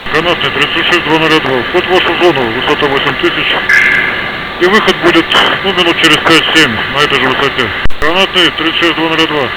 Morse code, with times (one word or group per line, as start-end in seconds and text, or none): none